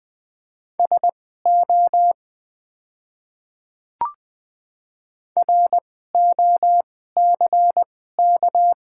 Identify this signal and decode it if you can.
Morse code, heard as a keyed tone, about 20 wpm, spaced as for 14 wpm, 690 Hz. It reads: SO ROCK
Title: 2x Character Spacing - Top 500 Words - 20wpm